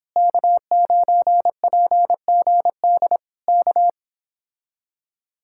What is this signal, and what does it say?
Morse code, heard as a keyed tone, about 26 wpm, 700 Hz. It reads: K9PGD K
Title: Speed-Racing Common QSO Phrases Encore - 22wpm